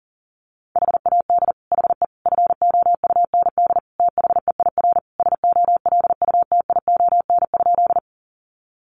Morse code, heard as a keyed tone, about 40 wpm, 700 Hz. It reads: HAD HE FOUND THEIR SOLUTION?
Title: Sentences from Top 1300 Words 40wpm